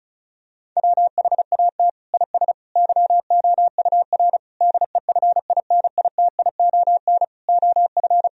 Morse code, heard as a keyed tone, about 36 wpm, 700 Hz. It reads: WHAT IS YOUR DEFINITION OF
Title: Sentences from Top 1000 Words - Review Part 4 - 35wpm